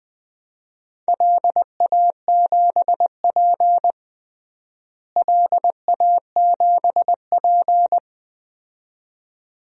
Morse code, heard as a keyed tone, about 20 wpm, 700 Hz. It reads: LA7P LA7P
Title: DXCC-100 Call Signs - 4 Characters - Repeated 2x - 20wpm